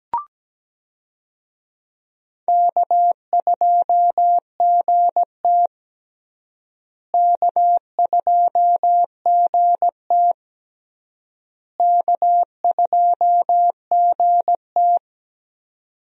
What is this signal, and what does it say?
Morse code, heard as a keyed tone, about 17 wpm, 700 Hz. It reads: K2GT K2GT K2GT
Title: USA Call Signs - 4 Characters - Repeated 3x - 17wpm